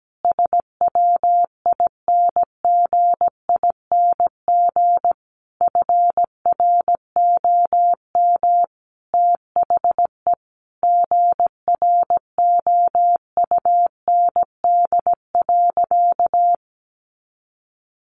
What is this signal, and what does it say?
Morse code, heard as a keyed tone, about 17 wpm, 700 Hz. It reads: SWINGING FROM THE GROUND.
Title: Easy Phrases 17wpm